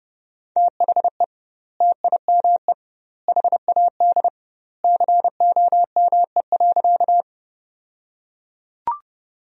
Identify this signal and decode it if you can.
Morse code, heard as a keyed tone, about 30 wpm, 700 Hz. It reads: THE TIME HAD COME.
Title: 2x Word-Spacing - Sentences from Top 100 Words - 30wpm